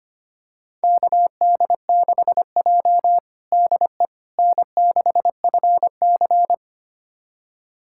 Morse code, heard as a keyed tone, about 25 wpm, 700 Hz. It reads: KD6J DE N6FC
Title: USA Callsign DE Callsign - 4 Characters - 25wpm